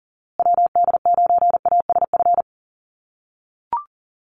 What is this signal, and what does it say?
Morse code, heard as a keyed tone, about 40 wpm, 700 Hz. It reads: WD9ASF
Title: USA Call Signs Repeated 2x 40wpm